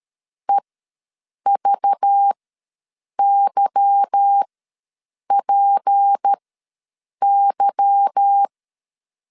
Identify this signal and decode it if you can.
Morse code, heard as a keyed tone, about 13 wpm, spaced as for 7 wpm, 800 Hz. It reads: EVYPY
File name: LEZIONI / 8 YJ45 / 8 Y ws 12wpm 3409